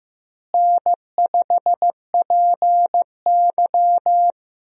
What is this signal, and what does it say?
Morse code, heard as a keyed tone, about 15 wpm, 700 Hz. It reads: N5PY
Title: USA Call Signs - 4 Characters - Repeated 3x - 15wpm